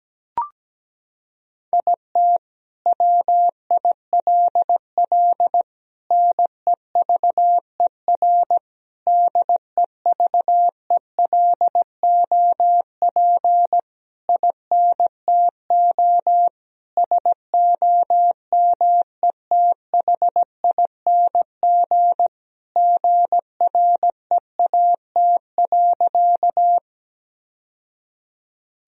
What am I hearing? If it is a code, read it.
Morse code, heard as a keyed tone, about 17 wpm, 700 Hz. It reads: IT WILL NEVER DEVELOP INTO SOMETHING GREAT.
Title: Sentences from Top 1000 Words - Review Part 2 - 17wpm